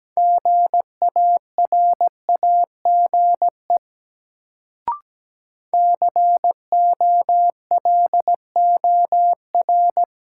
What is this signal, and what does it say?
Morse code, heard as a keyed tone, about 17 wpm, 700 Hz. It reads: GARAGE COLOR
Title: Sentences from Top 1600 Words - New Word Drill Alt - 17wpm